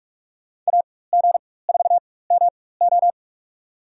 Morse code, heard as a keyed tone, about 44 wpm, spaced as for 21 wpm, 700 Hz. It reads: AG4MO
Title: USA Call Signs - 5 Characters - 4x Character Spacing - 45wpm